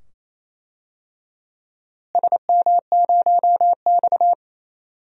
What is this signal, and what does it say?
Morse code, heard as a keyed tone, about 28 wpm, 700 Hz. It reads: SM0X
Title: DXCC-100 Call Signs - 4 Characters - Repeated 2x - 28wpm